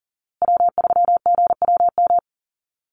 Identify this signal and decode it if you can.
Morse code, heard as a keyed tone, about 40 wpm, 700 Hz. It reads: W3GWM
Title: Call Signs Repeated 2x - 40wpm